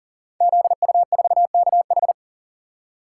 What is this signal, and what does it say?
Morse code, heard as a keyed tone, about 42 wpm, 700 Hz. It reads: ZU4KH
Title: Code Groups - 5 Letter:Numbers - 40wpm